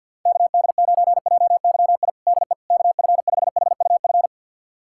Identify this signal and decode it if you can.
Morse code, heard as a keyed tone, about 50 wpm, 700 Hz. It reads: KD9JYI DE KF5HUF
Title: USA Callsign DE Callsign 50wpm